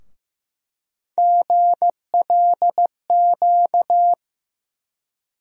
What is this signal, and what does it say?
Morse code, heard as a keyed tone, about 15 wpm, 700 Hz. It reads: GLQ
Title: USA Call Sign Postfixes - 3 Letters 15wpm